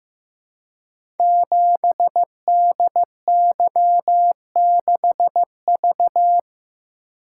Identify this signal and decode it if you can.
Morse code, heard as a keyed tone, about 15 wpm, 700 Hz. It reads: 7DY6V
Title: Code Groups - 5 Letter:Numbers - 15wpm